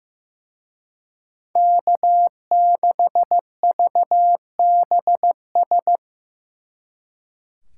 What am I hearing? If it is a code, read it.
Morse code, heard as a keyed tone, about 18 wpm, 700 Hz. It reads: K6VBS
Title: Speed Racing - USA 1x3 Call Signs 15wpm